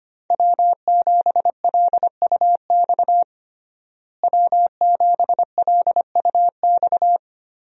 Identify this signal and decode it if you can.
Morse code, heard as a keyed tone, about 25 wpm, 700 Hz. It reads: W7LUX W7LUX
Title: USA Call Signs - 5 Characters - Repeated 3x - 25wpm